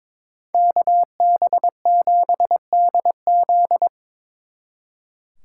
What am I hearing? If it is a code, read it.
Morse code, heard as a keyed tone, about 26 wpm, 700 Hz. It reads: KB7DZ
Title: Speed Racing - USA 2x2 Call Signs 22wpm